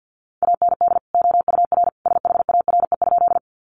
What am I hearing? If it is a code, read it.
Morse code, heard as a keyed tone, about 50 wpm, 700 Hz. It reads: AND OUR SHARE?
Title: Sentences from Top 900 Words 50wpm